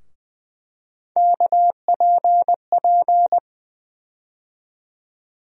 Morse code, heard as a keyed tone, about 20 wpm, 700 Hz. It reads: KPP
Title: USA Call Sign Postfixes - 3 Letters 20wpm